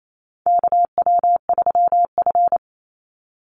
Morse code, heard as a keyed tone, about 28 wpm, 700 Hz. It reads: KW3F
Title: Call Signs - 28wpm